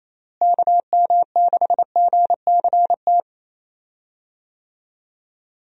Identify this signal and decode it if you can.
Morse code, heard as a keyed tone, about 28 wpm, 700 Hz. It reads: KM6GCT
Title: USA Call Signs - 6 Characters - Repeated 2x - 28wpm